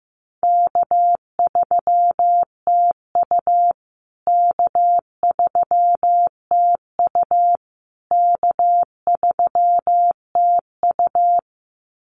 Morse code, heard as a keyed tone, about 15 wpm, 700 Hz. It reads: K3TU K3TU K3TU
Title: USA Call Signs Repeated 3x 15wpm